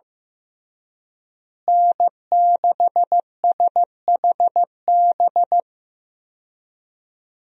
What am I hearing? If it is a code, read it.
Morse code, heard as a keyed tone, about 15 wpm, 700 Hz. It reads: N6SHB
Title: USA Call Signs - 5 Characters - Repeated 2x - 15wpm